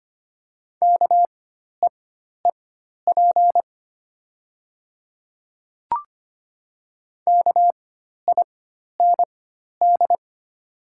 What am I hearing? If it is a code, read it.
Morse code, heard as a keyed tone, about 25 wpm, spaced as for 12 wpm, 700 Hz. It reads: KEEP KIND
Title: 4x Character Spacing - Top 500 Words - 25wpm